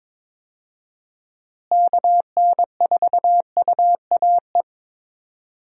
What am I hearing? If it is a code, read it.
Morse code, heard as a keyed tone, about 22 wpm, 700 Hz. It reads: KN4UAE
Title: USA Call Signs - 6 Characters - Repeated 3x - 22wpm